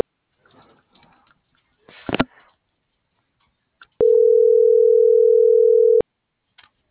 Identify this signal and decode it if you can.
Ringback tone (North America), 440 + 480 Hz.
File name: CENTRAL AGUILERA / out ram.lmorena.2025 - .12217